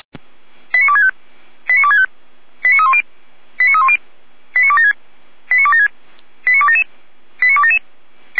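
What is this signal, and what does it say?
Five-tone selective call (ZVEI), heard as quick runs of five tones, tones of about 70 ms. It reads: 89265 89265 89310 89310 89276 89276 89277 89277
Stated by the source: ZVEI five-tone selective call received on an SDR.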